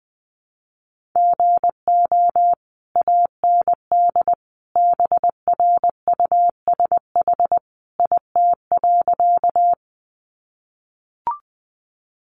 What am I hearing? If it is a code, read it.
Morse code, heard as a keyed tone, about 20 wpm, 700 Hz. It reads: GO AND BRUSH IT.
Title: Sentences from Top 1100 Words 20wpm